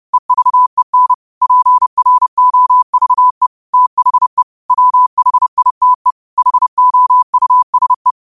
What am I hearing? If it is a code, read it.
Morse code, heard as a keyed tone, about 30 wpm, 1000 Hz. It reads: EVEN PROVE THE WHITE HOUSE
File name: qotdfiles / 30wpm